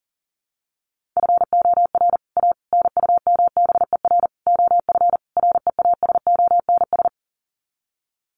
Morse code, heard as a keyed tone, about 40 wpm, 700 Hz. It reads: FOR A NUMBER OF REASONS
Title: Sets of 5 Words Encore2 40wpm